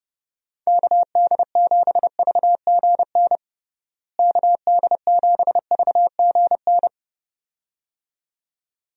Morse code, heard as a keyed tone, about 30 wpm, 700 Hz. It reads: KD7VGN KD7VGN
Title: USA Call Signs - 6 Characters - Repeated 2x - 30wpm